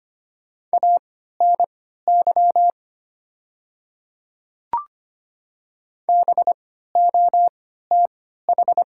Morse code, heard as a keyed tone, about 25 wpm, spaced as for 14 wpm, 700 Hz. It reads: ANY BOTH
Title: 3x Character Spacing - Top 300 Words - 25wpm